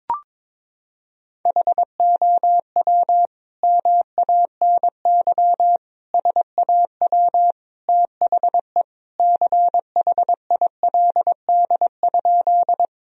Morse code, heard as a keyed tone, about 22 wpm, 700 Hz. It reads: HOW MANY SAW THE CHILD?
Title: Sentences from Top 1000 Words - Review Part 1 - 22wpm